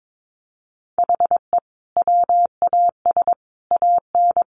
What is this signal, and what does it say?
Morse code, heard as a keyed tone, about 22 wpm, 700 Hz. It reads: HE WAS AN
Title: Sentences from Top 1200 Words 22wpm